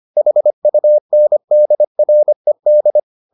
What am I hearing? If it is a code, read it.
Morse code, heard as a keyed tone, about 25 wpm, 600 Hz. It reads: HUNDRED